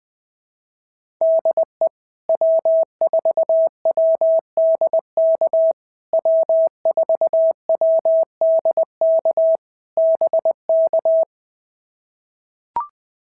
Morse code, heard as a keyed tone, about 18 wpm, 650 Hz. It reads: DE W4WDK W4WDK BK
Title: POTA QSO Practice - 20wpm